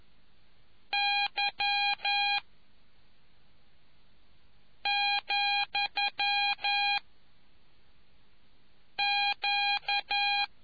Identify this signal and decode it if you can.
Morse code, heard as a keyed tone, about 11 wpm, 2450 Hz. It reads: Y , Q